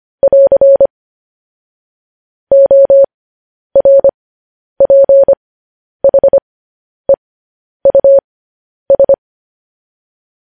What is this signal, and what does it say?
Morse code, heard as a keyed tone, about 25 wpm, spaced as for 10 wpm, 550 Hz. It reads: <AR> ORPHEUS
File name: podcast / News headlines-1x-25x10